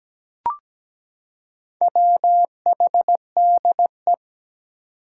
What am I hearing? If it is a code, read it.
Morse code, heard as a keyed tone, about 17 wpm, 700 Hz. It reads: WHDE
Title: Four Challenging Letter-Number Combinations 17wpm